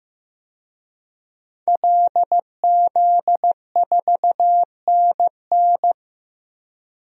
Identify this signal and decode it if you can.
Morse code, heard as a keyed tone, about 15 wpm, 700 Hz. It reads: LZ4NN
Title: DXCC-100 Call Signs - 5 Characters - 15wpm